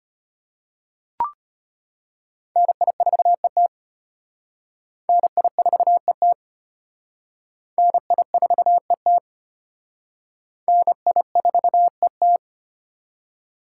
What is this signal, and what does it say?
Morse code, heard as a keyed tone, about 31 wpm, 700 Hz. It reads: NI4ET NI4ET NI4ET NI4ET